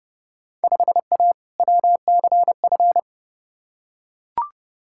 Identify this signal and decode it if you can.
Morse code, heard as a keyed tone, about 30 wpm, 700 Hz. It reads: 5A WCF
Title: Field Day Exchanges Encore - 30wpm